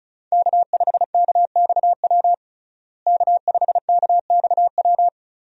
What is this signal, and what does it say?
Morse code, heard as a keyed tone, about 36 wpm, 700 Hz. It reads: K5KXW K5KXW
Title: USA Call Signs - 5 Characters - Repeated 3x - 35wpm